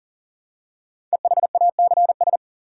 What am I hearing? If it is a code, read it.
Morse code, heard as a keyed tone, about 42 wpm, 700 Hz. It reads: EHACS
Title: Code Groups - 5 Letters - 40wpm